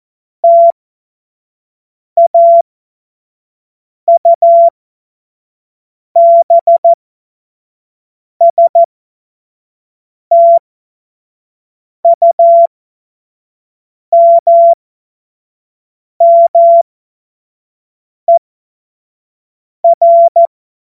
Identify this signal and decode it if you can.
Morse code, heard as a keyed tone, about 14 wpm, spaced as for 5 wpm, 680 Hz. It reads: TAUBSTUMMER
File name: Morsekurs / Witze / Witz 02. Mai 2023 25 F70